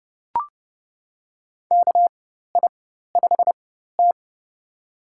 Morse code, heard as a keyed tone, about 29 wpm, spaced as for 14 wpm, 700 Hz. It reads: KI5T
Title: USA Call Signs - 4 Characters - 4x Character Spacing - 30wpm